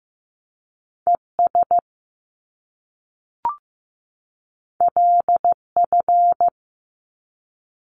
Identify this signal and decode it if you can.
Morse code, heard as a keyed tone, about 15 wpm, 700 Hz. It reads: ES LF
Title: Top Two Letter Combinations in Words 15wpm